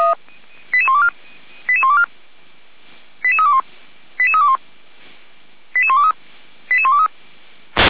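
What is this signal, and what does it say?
Five-tone selective call (ZVEI), heard as quick runs of five tones, tones of about 70 ms. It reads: 88124 88124 88321 88321 88123 88123
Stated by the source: ZVEI five-tone selective call received via SDR.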